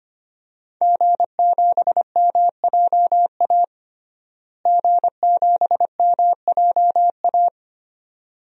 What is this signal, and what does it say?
Morse code, heard as a keyed tone, about 25 wpm, 700 Hz. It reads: G7MJA G7MJA
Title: DXCC-100 Call Signs - 5 Characters - Repeated 2x - 25wpm